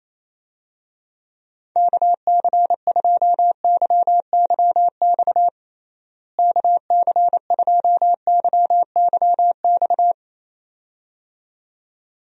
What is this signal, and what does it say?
Morse code, heard as a keyed tone, about 28 wpm, 700 Hz. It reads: KC2YYX KC2YYX
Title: USA Call Signs - 6 Characters - Repeated 2x - 28wpm